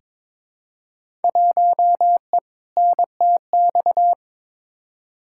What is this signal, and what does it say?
Morse code, heard as a keyed tone, about 21 wpm, 700 Hz. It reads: 1E NTX
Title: Field Day Exchanges - 22wpm